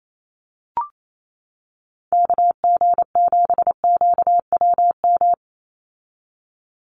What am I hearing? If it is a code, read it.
Morse code, heard as a keyed tone, about 28 wpm, 700 Hz. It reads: KG7QWM
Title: USA Call Signs 28wpm